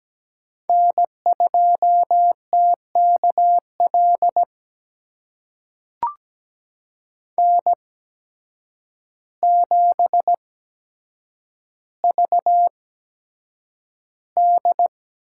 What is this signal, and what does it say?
Morse code, heard as a keyed tone, about 17 wpm, spaced as for 8 wpm, 700 Hz. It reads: N2TKL N7VD
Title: USA Call Signs - 8x Character Spacing - 17wpm